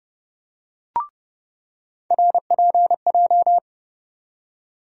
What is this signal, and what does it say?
Morse code, heard as a keyed tone, about 30 wpm, 700 Hz. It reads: RPJ
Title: USA Call Sign Postfixes - 3 Letters 30wpm